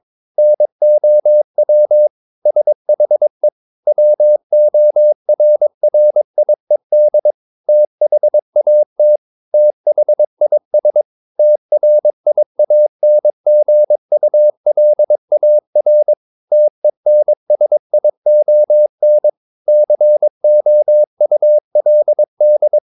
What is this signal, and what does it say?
Morse code, heard as a keyed tone, about 22 wpm, 600 Hz. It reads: NOW SHE WORRIED THAT THIS TRIANGULAR TENSION COULD
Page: Romanen Never af Ken Follet på engelsk.